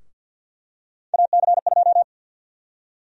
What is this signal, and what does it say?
Morse code, heard as a keyed tone, about 48 wpm, 700 Hz. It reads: AC2
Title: USA Call Sign Prefixes - 50wpm